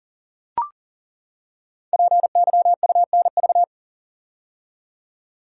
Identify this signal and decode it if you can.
Morse code, heard as a keyed tone, about 42 wpm, 700 Hz. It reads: PYUNV
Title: Code Groups - 5 Letters - 40wpm